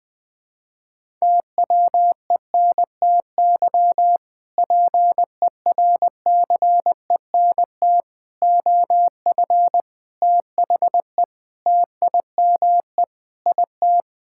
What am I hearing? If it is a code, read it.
Morse code, heard as a keyed tone, about 20 wpm, 700 Hz. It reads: TWENTY PERCENT OF THE TIME IT
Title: Sentences with 2-Word Phrases - 20wpm